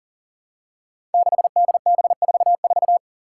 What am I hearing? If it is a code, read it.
Morse code, heard as a keyed tone, about 38 wpm, 700 Hz. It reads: 6DB44
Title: Five Challenging Letter-Number Combinations 40wpm